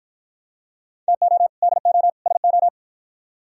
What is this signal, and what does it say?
Morse code, heard as a keyed tone, about 52 wpm, 700 Hz. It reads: TO DO SO